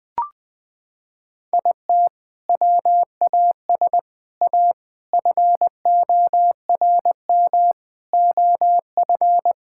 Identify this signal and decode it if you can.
Morse code, heard as a keyed tone, about 20 wpm, 700 Hz. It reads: IT WAS A FORM OF
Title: Sentences from Top 1000 Words - Review Part 5 - 20wpm